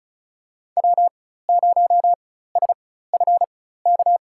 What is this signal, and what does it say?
Morse code, heard as a keyed tone, about 34 wpm, spaced as for 16 wpm, 700 Hz. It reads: W0SFK
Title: USA Call Signs - 4x Character Spacing - 35wpm